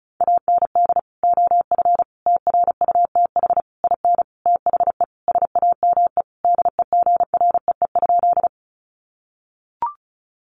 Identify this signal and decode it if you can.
Morse code, heard as a keyed tone, about 34 wpm, 700 Hz. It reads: AND OF TRUTH IN THE SAME DEGREE?
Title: Sentences from Top 1500 Words 35wpm